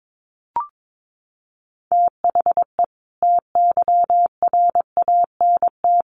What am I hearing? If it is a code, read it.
Morse code, heard as a keyed tone, about 22 wpm, 700 Hz. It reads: THE TYRANT
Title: A Sound Of Thunder 22wpm